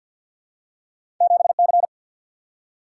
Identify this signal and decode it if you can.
Morse code, heard as a keyed tone, about 53 wpm, 700 Hz. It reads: 7C
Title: Single-Letter Single-Number 50wpm